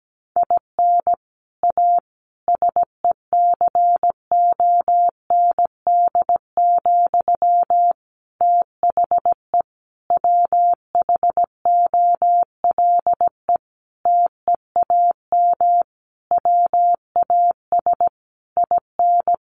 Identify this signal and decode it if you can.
Morse code, heard as a keyed tone, about 17 wpm, 700 Hz. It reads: IN A SECOND, THE WHOLE TEAM WAS IN
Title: Sentences from Top 1500 Words 17wpm